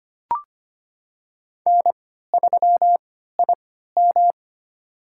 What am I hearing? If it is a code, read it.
Morse code, heard as a keyed tone, about 25 wpm, spaced as for 14 wpm, 700 Hz. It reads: N3IM
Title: USA Call Signs - 3x Character Spacing - 25wpm